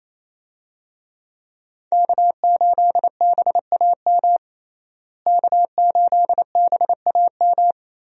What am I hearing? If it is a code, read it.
Morse code, heard as a keyed tone, about 27 wpm, 700 Hz. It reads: K8BAM K8BAM
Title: USA Call Signs - 5 Characters - Repeated 2x - 28wpm